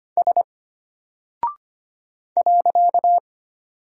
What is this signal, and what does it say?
Morse code, heard as a keyed tone, about 25 wpm, 700 Hz. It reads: S .
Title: Single Character (Letter:Number:Punctuation:Prosign) Alt 25wpm